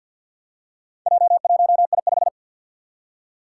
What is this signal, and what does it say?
Morse code, heard as a keyed tone, about 49 wpm, 700 Hz. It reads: J1I5
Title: Four Challenging Letter-Number Combinations 50wpm